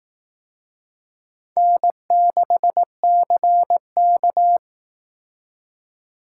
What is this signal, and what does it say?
Morse code, heard as a keyed tone, about 18 wpm, 700 Hz. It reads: N6CK
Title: Speed Racing - USA 1x2 Call Signs 15wpm